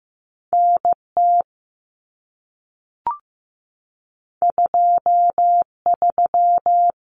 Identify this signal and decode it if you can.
Morse code, heard as a keyed tone, about 15 wpm, 700 Hz. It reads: NT 23
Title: Common Cut Numbers - 2-Digit 15wpm